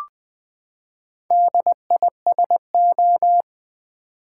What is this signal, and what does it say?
Morse code, heard as a keyed tone, about 20 wpm, 700 Hz. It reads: DISO
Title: Build A Word - 4 Letters At A Time - 8-Letter Words - 20wpm